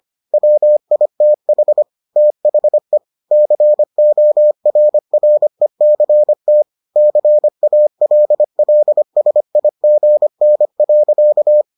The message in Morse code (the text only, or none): WITH THE CORRECT CALLSIGN.